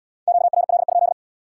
Morse code, heard as a keyed tone, about 76 wpm, 700 Hz. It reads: QRL?